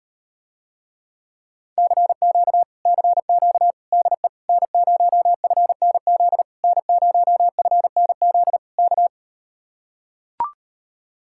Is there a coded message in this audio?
Yes: Morse code from 1.8 to 9.1 s, heard as a keyed tone, about 38 wpm, 700 Hz. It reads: CQ CQ DE N0FNZ N0FNZ K